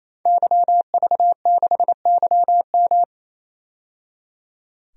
Morse code, heard as a keyed tone, about 28 wpm, 700 Hz. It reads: YV6YM
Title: DXCC-100 Call Signs - 5 Characters - 28wpm